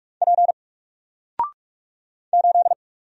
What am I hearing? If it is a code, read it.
Morse code, heard as a keyed tone, about 44 wpm, 700 Hz. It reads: P 8
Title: Single Character (Letter:Number:Punctuation:Prosign) 45wpm